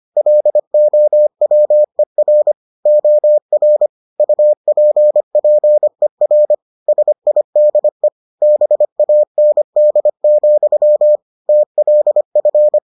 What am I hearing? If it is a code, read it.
Morse code, heard as a keyed tone, about 25 wpm, 600 Hz. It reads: LOWER OR UPPER SIDE BAND, TLF